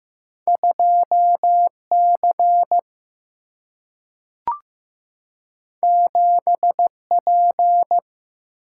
Morse code, heard as a keyed tone, about 15 wpm, 700 Hz. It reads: 2C 7P
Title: Single-Letter Single-Number 15wpm